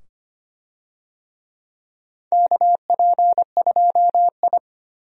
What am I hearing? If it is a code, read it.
Morse code, heard as a keyed tone, about 25 wpm, 700 Hz. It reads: KP2I
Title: DXCC-100 Call Signs - 4 Characters - 25wpm